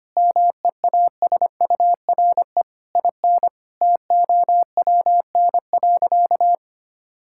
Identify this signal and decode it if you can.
Morse code, heard as a keyed tone, about 25 wpm, 700 Hz. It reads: MEASURE IN TOWN.
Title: Sentences from Top 1000 Words - Review Part 5 - 25wpm